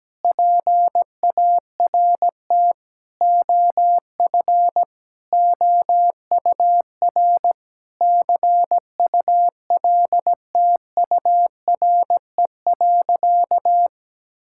Morse code, heard as a keyed tone, about 17 wpm, 700 Hz. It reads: PART OF OUR CULTURE.
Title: Sentences from Top 1000 Words - Review Part 6 - 17wpm